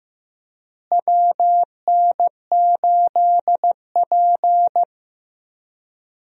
Morse code, heard as a keyed tone, about 15 wpm, 700 Hz. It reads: WN8P
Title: USA Call Signs - 4 Characters - Repeated 3x - 15wpm